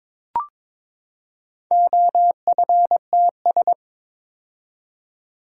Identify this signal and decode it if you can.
Morse code, heard as a keyed tone, about 22 wpm, 700 Hz. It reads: OFTS
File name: practice / 22wpm / Four Letter-Number Combinations - 22wpm